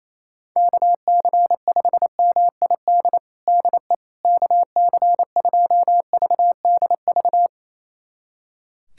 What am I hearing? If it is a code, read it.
Morse code, heard as a keyed tone, about 27 wpm, 700 Hz. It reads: KC5MID DE KC2VDV
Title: USA Callsign DE Callsign - 6 Characters - 28wpm